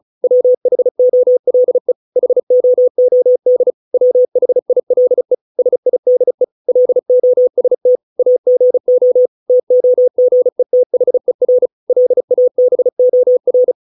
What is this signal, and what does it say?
Morse code, heard as a keyed tone, about 36 wpm, 500 Hz. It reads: WHOLE HOOD WHILE SIDE LOST AGO TOGETHER LABOR
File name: 632 words / 632 Words 35WPM